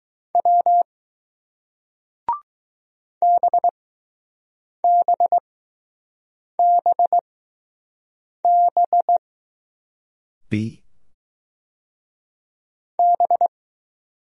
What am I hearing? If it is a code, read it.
Morse code, heard as a keyed tone, about 20 wpm, spaced as for 6 wpm, 700 Hz. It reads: W BBBB B